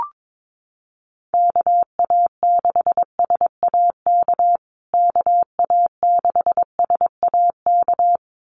Morse code, heard as a keyed tone, about 22 wpm, 700 Hz. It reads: KA6SAK KA6SAK
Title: USA Call Signs Repeated 2x 22wpm